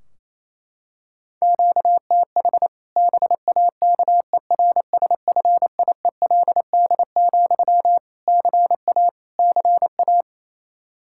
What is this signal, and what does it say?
Morse code, heard as a keyed tone, about 28 wpm, 700 Hz. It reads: QTH BAKERSFIELD, CA CA
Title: Practice QSO - Exchange 1 - Beginning - 28wpm